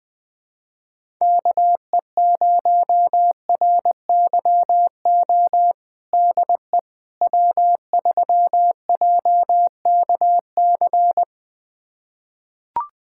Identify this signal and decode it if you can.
Morse code, heard as a keyed tone, about 20 wpm, 700 Hz. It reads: KE0RYO DE W3JKC
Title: USA Callsign DE Callsign 20wpm